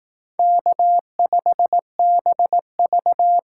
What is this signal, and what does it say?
Morse code, heard as a keyed tone, about 18 wpm, 700 Hz. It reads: K5BV
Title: Speed Racing - USA Call Signs 15wpm